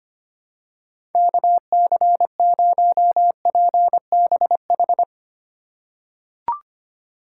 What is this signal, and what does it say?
Morse code, heard as a keyed tone, about 25 wpm, 700 Hz. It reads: KC0PBH
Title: USA Call Signs - 2x Character Spacing - 25wpm